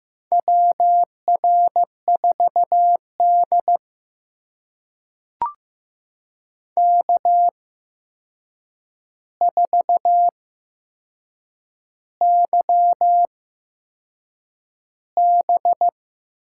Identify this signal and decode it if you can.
Morse code, heard as a keyed tone, about 15 wpm, spaced as for 6 wpm, 700 Hz. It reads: WR4D K4YB
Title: USA Call Signs - 4 Characters - 8x Character Spacing - 15wpm